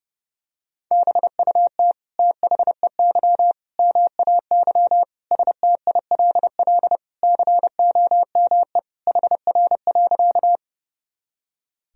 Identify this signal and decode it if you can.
Morse code, heard as a keyed tone, about 30 wpm, 700 Hz. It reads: BUT THEY MAY STILL COME HR.
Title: Sentences from Top 300 Words and Abbreviations 30wpm